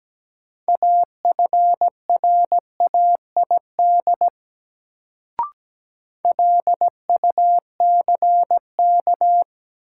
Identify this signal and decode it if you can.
Morse code, heard as a keyed tone, about 17 wpm, 700 Hz. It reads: AFRAID LUCK
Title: Sentences from Top 1500 Words - New Word Drill - 17wpm